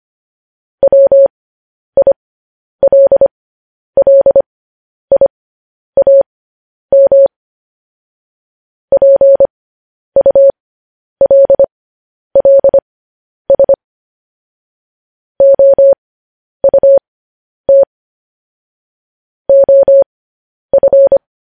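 Morse code, heard as a keyed tone, about 25 wpm, spaced as for 10 wpm, 550 Hz. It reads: WILLIAM PULLS OUT OF
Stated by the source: Morse code news headlines practice at 10 wpm.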